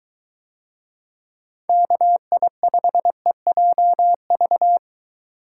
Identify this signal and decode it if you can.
Morse code, heard as a keyed tone, about 21 wpm, 700 Hz. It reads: KI5EJV